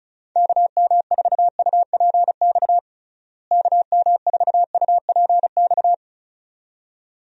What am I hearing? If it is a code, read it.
Morse code, heard as a keyed tone, about 34 wpm, 700 Hz. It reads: KM4UPX KM4UPX
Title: USA Call Signs - 6 Characters - Repeated 2x - 35wpm